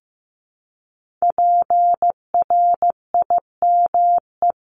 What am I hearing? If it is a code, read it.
Morse code, heard as a keyed tone, about 16 wpm, 700 Hz. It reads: PRIME
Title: Multi-letter Phonemes Repeated 3x 15wpm